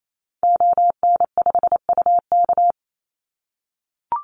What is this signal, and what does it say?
Morse code, heard as a keyed tone, about 28 wpm, 700 Hz. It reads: ON5UK
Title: Call Signs - 28wpm